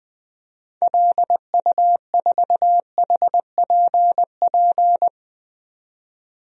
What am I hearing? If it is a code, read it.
Morse code, heard as a keyed tone, about 21 wpm, 700 Hz. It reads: LU4HPP
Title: DXCC-100 Call Signs - 6 Characters - 20wpm